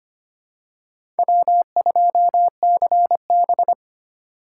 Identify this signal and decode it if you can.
Morse code, heard as a keyed tone, about 25 wpm, 700 Hz. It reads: W2CB
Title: USA Call Signs - 4 Characters - 2x Character Spacing - 25wpm